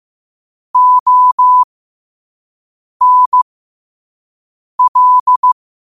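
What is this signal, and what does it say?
Morse code, heard as a keyed tone, about 15 wpm, spaced as for 5 wpm, 1000 Hz. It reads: ONL
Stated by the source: Morse code sent at 5 wpm.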